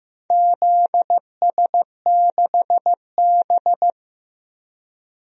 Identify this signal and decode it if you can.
Morse code, heard as a keyed tone, about 15 wpm, 700 Hz. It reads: ZS6B
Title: DXCC-100 Call Signs - 4 Characters - Repeated 3x - 15wpm